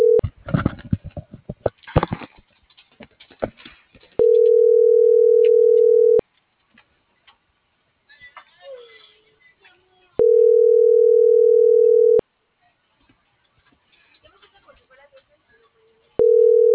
Ringback tone (North America), 440 + 480 Hz.